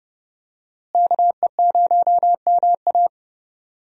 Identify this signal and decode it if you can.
Morse code, heard as a keyed tone, about 31 wpm, 700 Hz. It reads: KE0MA
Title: USA Call Signs - 5 Characters - Repeated 3x - 30wpm